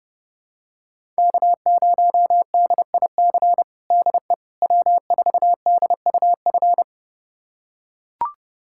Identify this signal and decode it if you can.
Morse code, heard as a keyed tone, about 30 wpm, 700 Hz. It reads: K0DIC DE W4DUF
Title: USA Callsign DE Callsign - 5 Characters - 30wpm